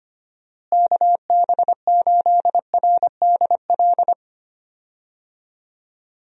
Morse code, heard as a keyed tone, about 25 wpm, 700 Hz. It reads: KB8RDL
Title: USA Call Signs - 6 Characters - Repeated 2x - 25wpm